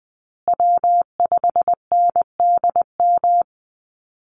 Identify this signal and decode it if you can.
Morse code, heard as a keyed tone, about 20 wpm, 700 Hz. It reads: W5NDM
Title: USA Call Signs Repeated 2x 20wpm